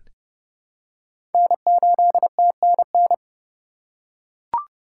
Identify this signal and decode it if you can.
Morse code, heard as a keyed tone, about 32 wpm, 700 Hz. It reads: N8TNN
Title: USA Call Signs - 5 Characters - 4x Character Spacing - 30wpm